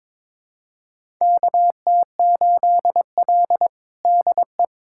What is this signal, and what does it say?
Morse code, heard as a keyed tone, about 21 wpm, 700 Hz. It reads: KT8L DE
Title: USA Callsign DE Callsign - 4 Characters - 22wpm